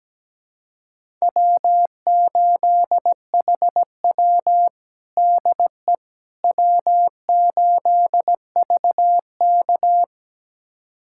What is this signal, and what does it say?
Morse code, heard as a keyed tone, about 17 wpm, 700 Hz. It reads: W8HW DE W8VK
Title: USA Callsign DE Callsign - 4 Characters - 17wpm